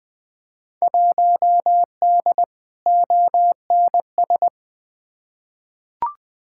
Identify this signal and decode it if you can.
Morse code, heard as a keyed tone, about 20 wpm, 700 Hz. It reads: 1D ONS
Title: Field Day Exchanges - 20wpm